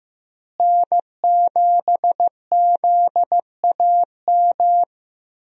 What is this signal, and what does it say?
Morse code, heard as a keyed tone, about 15 wpm, 700 Hz. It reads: N7ZAM
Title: USA Call Signs - 5 Characters - Repeated 3x - 15wpm